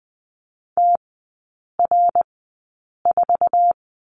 Morse code, heard as a keyed tone, about 20 wpm, spaced as for 8 wpm, 700 Hz. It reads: TR4
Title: 060 Lesson 11e - Letter h - Three Characters 20wpm